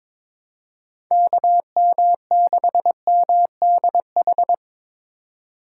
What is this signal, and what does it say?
Morse code, heard as a keyed tone, about 21 wpm, 700 Hz. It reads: KM6MDH
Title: USA Call Signs - 6 Characters - 3x Character Spacing - 22wpm